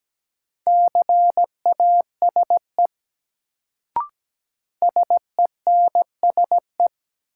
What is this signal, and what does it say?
Morse code, heard as a keyed tone, about 17 wpm, 700 Hz. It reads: CASE SENSE
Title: Sentences from Top 700 Words - New Word Drill - 17wpm